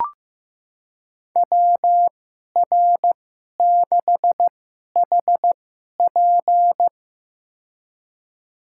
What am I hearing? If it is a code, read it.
Morse code, heard as a keyed tone, about 15 wpm, spaced as for 11 wpm, 700 Hz. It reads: WR6HP
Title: USA Call Signs - 2x Character Spacing - 15wpm